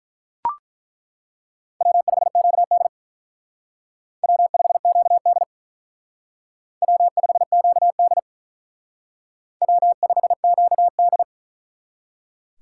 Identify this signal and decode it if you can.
Morse code, heard as a keyed tone, about 42 wpm, 700 Hz. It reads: W5QD W5QD W5QD W5QD
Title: Speed Racing - USA Call Signs 35wpm